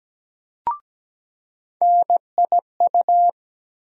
Morse code, heard as a keyed tone, about 17 wpm, 700 Hz. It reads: NIU
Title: USA Call Sign Postfixes - 3 Letters 17wpm